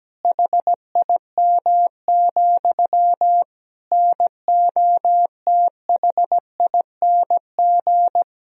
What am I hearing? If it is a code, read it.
Morse code, heard as a keyed tone, about 17 wpm, 700 Hz. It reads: HIM, NOTHING
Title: Sentences from Top 1000 Words - Review Part 1 - 17wpm